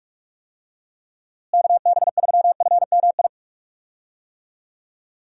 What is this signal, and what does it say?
Morse code, heard as a keyed tone, about 43 wpm, 700 Hz. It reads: KB3FMI
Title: USA Call Signs - 6 Characters - Repeated 2x - 45wpm